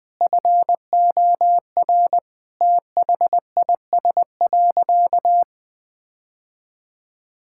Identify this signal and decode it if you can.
Morse code, heard as a keyed tone, about 20 wpm, 700 Hz. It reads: FOR THIS.
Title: Sentences with 5-Word Phrases - Spoken Beginning - 20wpm